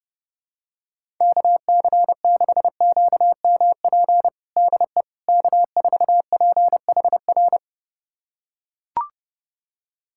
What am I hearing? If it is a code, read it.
Morse code, heard as a keyed tone, about 31 wpm, 700 Hz. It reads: KC6QMP DE K4PHR
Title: USA Callsign DE Callsign 30wpm